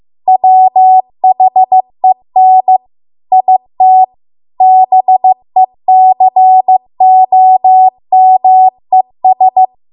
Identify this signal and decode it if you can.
Morse code, heard as a keyed tone, about 15 wpm, 750 Hz. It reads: WHEN IT BECOMES